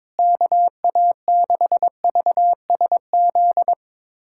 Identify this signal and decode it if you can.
Morse code, heard as a keyed tone, about 22 wpm, 700 Hz. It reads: KA6VSZ
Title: USA Callsign DE Callsign - 6 Characters - 22wpm